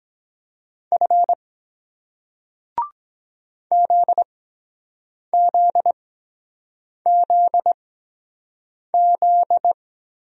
Morse code, heard as a keyed tone, about 23 wpm, spaced as for 7 wpm, 700 Hz. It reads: F ZZZZ